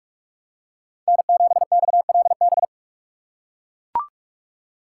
Morse code, heard as a keyed tone, about 42 wpm, 700 Hz. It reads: N7XLB
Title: USA Call Signs - 5 Characters - 4x Character Spacing - 45wpm